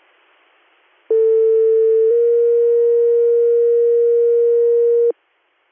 Two-tone page, heard as a one-second tone, then a three-second tone, about 450 then 470 Hz. It is heard from 1.1 to 5.1 s.